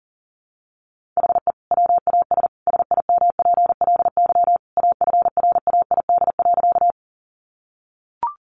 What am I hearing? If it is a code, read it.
Morse code, heard as a keyed tone, about 40 wpm, 700 Hz. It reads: HE WAS SIMPLY AFRAID.
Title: Sentences from Top 1500 Words 40wpm